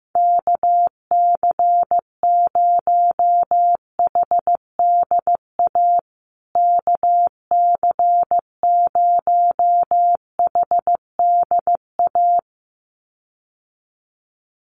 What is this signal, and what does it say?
Morse code, heard as a keyed tone, about 15 wpm, 700 Hz. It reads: KC0HDA KC0HDA
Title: USA Call Signs Repeated 2x 15wpm